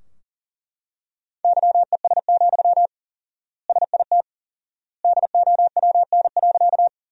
Morse code, heard as a keyed tone, about 40 wpm, 700 Hz. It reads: YES, SIT DOWN.
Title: 4x Word-Spacing - Sentences from Top 500 Words - 40wpm